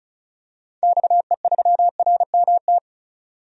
Morse code, heard as a keyed tone, about 34 wpm, 700 Hz. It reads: XE3RMT
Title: DXCC-100 Call Signs - 6 Characters - 35wpm